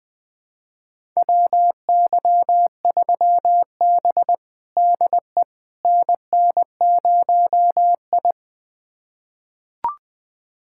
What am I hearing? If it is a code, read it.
Morse code, heard as a keyed tone, about 20 wpm, 700 Hz. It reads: WY3B DE NN0I
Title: USA Callsign DE Callsign - 4 Characters - 20wpm